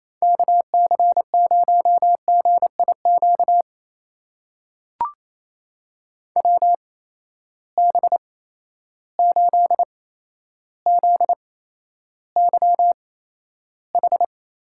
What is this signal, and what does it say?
Morse code, heard as a keyed tone, about 28 wpm, spaced as for 12 wpm, 700 Hz. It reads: KC0GIQ WB8ZYH
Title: USA Call Signs - 8x Character Spacing - 28wpm